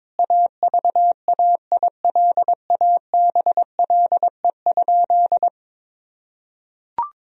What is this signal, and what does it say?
Morse code, heard as a keyed tone, about 22 wpm, 700 Hz. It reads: AVAILABLE?
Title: Sentences from Top 1000 Words - Review Part 3 - 22wpm